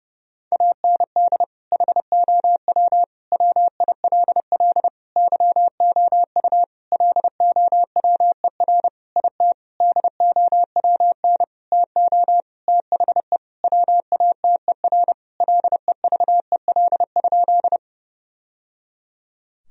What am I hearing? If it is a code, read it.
Morse code, heard as a keyed tone, about 30 wpm, 700 Hz. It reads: AND HOW WILL YOU LOWER IT DOWN TO THE WATER LEVEL?
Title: Sentences from Top 1000 Words - Review Part 4 - 30wpm